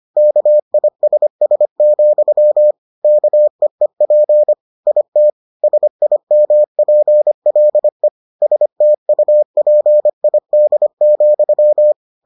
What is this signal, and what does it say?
Morse code, heard as a keyed tone, about 25 wpm, 600 Hz. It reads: KISS, KEEP IT SIMPLE STUPID,